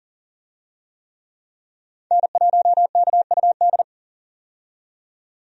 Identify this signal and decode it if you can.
Morse code, heard as a keyed tone, about 40 wpm, 700 Hz. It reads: N1KUD
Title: USA Call Signs - 5 Characters - 40wpm